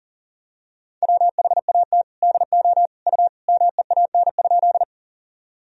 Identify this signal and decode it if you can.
Morse code, heard as a keyed tone, about 40 wpm, 700 Hz. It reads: WHAT DO U MEAN?
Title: Sentences from Top 200 Words and Abbreviations 40wpm